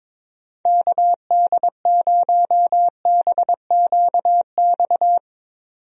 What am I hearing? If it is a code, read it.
Morse code, heard as a keyed tone, about 22 wpm, 700 Hz. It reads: KD0BQX
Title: USA Call Signs - 6 Characters - 22wpm